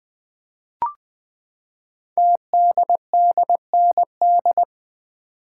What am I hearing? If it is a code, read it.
Morse code, heard as a keyed tone, about 20 wpm, 700 Hz. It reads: TDDND
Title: Code Groups - 5 Letters - 20wpm